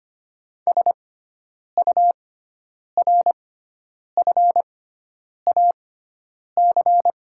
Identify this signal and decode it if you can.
Morse code, heard as a keyed tone, about 25 wpm, spaced as for 9 wpm, 690 Hz. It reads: SURFAC
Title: 6x Character Spacing - Top 500 Words - 25wpm